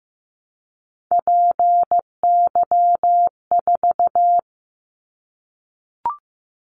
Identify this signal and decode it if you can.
Morse code, heard as a keyed tone, about 15 wpm, 700 Hz. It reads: PY4
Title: Callsign Prefixes for Top 100 DXCC - 15wpm